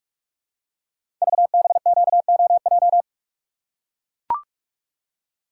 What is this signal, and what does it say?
Morse code, heard as a keyed tone, about 44 wpm, 690 Hz. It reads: VBQOJ
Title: Code Groups - 5 Letters - 45wpm